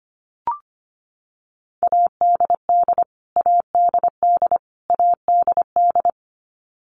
Morse code, heard as a keyed tone, about 25 wpm, 700 Hz. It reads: ADD ADD ADD
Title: Top 500 Words Repeated 25wpm